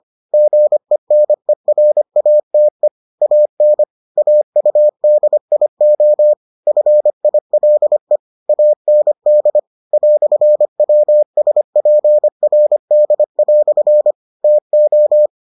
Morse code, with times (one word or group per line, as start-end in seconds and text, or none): 0.3-2.9 GENERATE
3.2-3.8 AN
4.2-6.3 AUDIO
6.7-8.2 FILE
8.5-9.6 AND
9.9-14.1 "WSPRD"
14.4-15.3 TO